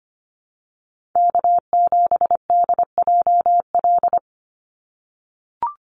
Morse code, heard as a keyed tone, about 25 wpm, 700 Hz. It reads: K7DJL
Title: NA SOTA Chaser Callsigns 25wpm